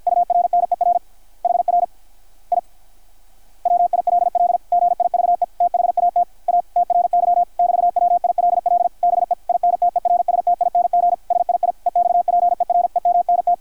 Morse code, heard as a keyed tone, about 52 wpm, 660 Hz, with radio noise. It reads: URNER HR I WILL GIVE THAT A TRY <BT> WILL BE INTERSTING SII EXPEREMNT
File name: archived / CRSnet